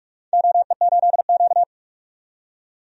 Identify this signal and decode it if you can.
Morse code, heard as a keyed tone, about 45 wpm, 700 Hz. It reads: OE8Q
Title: DXCC-100 Call Signs - 4 Characters - 45wpm